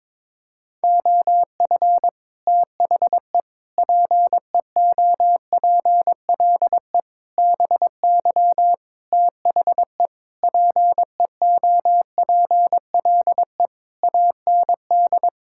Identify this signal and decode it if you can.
Morse code, heard as a keyed tone, about 22 wpm, 700 Hz. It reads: OF THE PEOPLE BY THE PEOPLE AND
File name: practice / 22wpm / Sets of 8 Words - 22wpm